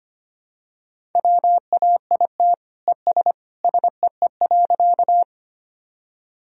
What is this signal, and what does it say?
Morse code, heard as a keyed tone, about 25 wpm, 700 Hz. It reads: WAIT ES SEE.
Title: Sentences from Top 500 Words and Abbreviations 25wpm